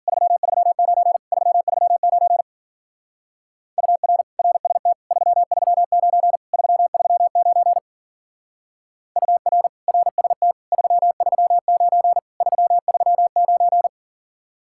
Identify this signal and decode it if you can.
Morse code, heard as a keyed tone, about 48 wpm, 700 Hz. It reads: 339 339 UR RST 339 339 UR RST 339 339
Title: Speed-Racing Common QSO Phrases Encore - 40wpm